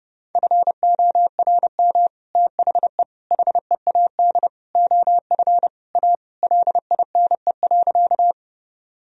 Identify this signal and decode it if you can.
Morse code, heard as a keyed tone, about 32 wpm, 700 Hz. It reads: FORM THE HEAD OF A LINE.
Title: Sentences from Top 1000 Words - Review Part 1 - 30wpm